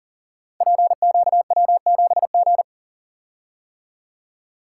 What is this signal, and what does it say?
Morse code, heard as a keyed tone, about 42 wpm, 700 Hz. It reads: PQW7G
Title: Code Groups - 5 Letter:Numbers - 40wpm